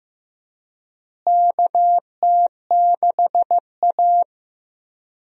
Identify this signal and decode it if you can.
Morse code, heard as a keyed tone, about 15 wpm, 700 Hz. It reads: KT6A
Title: USA Call Signs - 4 Characters - Repeated 2x - 15wpm